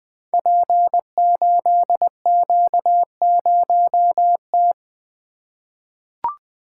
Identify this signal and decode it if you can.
Morse code, heard as a keyed tone, about 20 wpm, 700 Hz. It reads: P8Q0T
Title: Code Groups - 5 Letter:Numbers - 20wpm